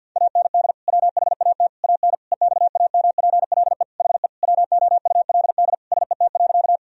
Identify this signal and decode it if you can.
Morse code, heard as a keyed tone, about 51 wpm, 700 Hz. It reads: AND WHAT AN EXAMPLE HE WOULD SET.
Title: Sentences from Top 1000 Words - Review Part 1 - 50wpm